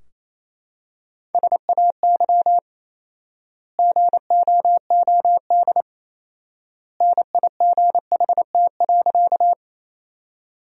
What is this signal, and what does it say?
Morse code, heard as a keyed tone, about 27 wpm, 700 Hz. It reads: SAY GOOD NIGHT.
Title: 4x Word-Spacing - Sentences from Top 300 Words - 28wpm